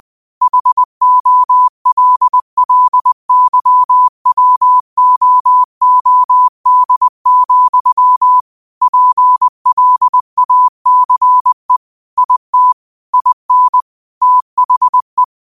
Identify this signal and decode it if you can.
Morse code, heard as a keyed tone, about 20 wpm, 1000 Hz. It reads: HOLLYWOOD, PLACE IT IN THE